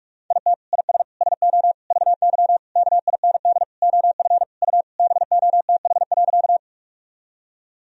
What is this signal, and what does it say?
Morse code, heard as a keyed tone, about 46 wpm, 700 Hz. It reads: IT IS SO VY KIND OF U BOTH.
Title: Sentences from Top 300 Words and Abbreviations 45wpm